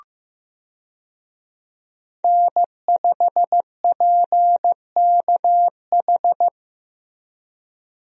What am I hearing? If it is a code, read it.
Morse code, heard as a keyed tone, about 15 wpm, 700 Hz. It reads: N5PKH
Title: USA Call Signs - 5 Characters - Repeated 2x - 15wpm